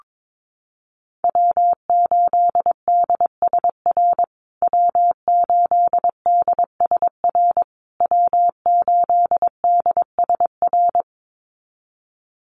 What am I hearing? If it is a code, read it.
Morse code, heard as a keyed tone, about 22 wpm, 700 Hz. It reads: W8DSR W8DSR W8DSR
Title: USA Call Signs Repeated 3x 22wpm